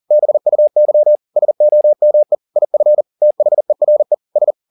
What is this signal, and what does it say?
Morse code, heard as a keyed tone, about 40 wpm, 600 Hz. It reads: BUY SOME IF THERE S
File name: the old man and the sea 2 40wpm